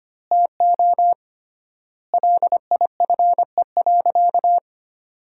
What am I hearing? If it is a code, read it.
Morse code, heard as a keyed tone, about 25 wpm, 700 Hz. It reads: TO LIFE.
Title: 3x Word-Spacing - Sentences from Top 400 Words - 25wpm